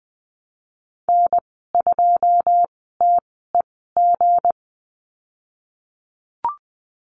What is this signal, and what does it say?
Morse code, heard as a keyed tone, about 20 wpm, spaced as for 15 wpm, 700 Hz. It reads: N2TEG
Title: 161 Lesson 26g - Letter g - Call Signs 20wpm